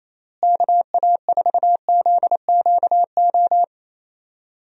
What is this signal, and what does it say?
Morse code, heard as a keyed tone, about 28 wpm, 700 Hz. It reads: KA4ZQO
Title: USA Call Signs - 6 Characters - Repeated 3x - 28wpm